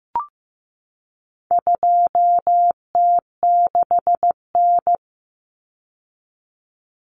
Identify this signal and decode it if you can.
Morse code, heard as a keyed tone, about 15 wpm, 700 Hz. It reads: 2T6N
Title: Common Cut Numbers - 4-Digit 15wpm